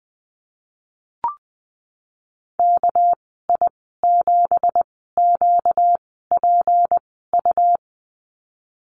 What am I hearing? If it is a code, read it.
Morse code, heard as a keyed tone, about 20 wpm, spaced as for 15 wpm, 700 Hz. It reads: KI7QPU
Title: 245 Lesson 39g - Letter z - Call Signs 20wpm